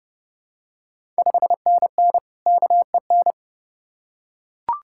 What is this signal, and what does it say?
Morse code, heard as a keyed tone, about 30 wpm, 700 Hz. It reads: 5NN KEN
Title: Ontario QSO Party Exchange 30wpm